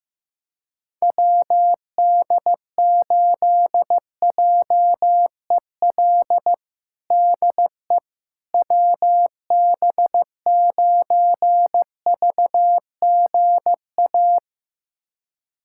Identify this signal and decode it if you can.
Morse code, heard as a keyed tone, about 16 wpm, 700 Hz. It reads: WD8JEL DE WB9VGA
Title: USA Callsign DE Callsign - 6 Characters - 15wpm